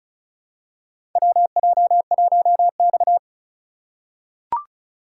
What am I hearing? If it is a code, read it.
Morse code, heard as a keyed tone, about 36 wpm, 700 Hz. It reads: WJ1X
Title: USA Call Signs - 2x Character Spacing - 35wpm